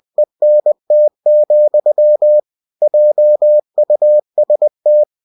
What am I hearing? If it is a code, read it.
Morse code, heard as a keyed tone, about 20 wpm, 600 Hz. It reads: ENT, JUST